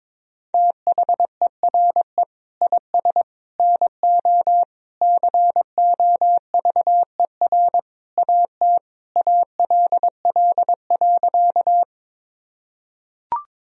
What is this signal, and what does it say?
Morse code, heard as a keyed tone, about 22 wpm, 700 Hz. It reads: THERE IS NO COVER AT ALL.
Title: Sentences from Top 1000 Words - Review Part 1 - 22wpm